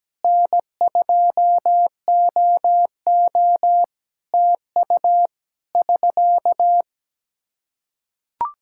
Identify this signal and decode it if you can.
Morse code, heard as a keyed tone, about 17 wpm, 700 Hz. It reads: N2OO TU <SK>
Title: Practice QSO - Exchange 4 - Ending - 17wpm